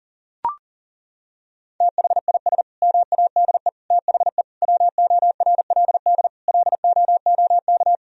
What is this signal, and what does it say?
Morse code, heard as a keyed tone, about 37 wpm, 700 Hz. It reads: THIS MADE THE WORLD LOOK
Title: Sentences from Top 1000 Words - Review Part 2 - 40wpm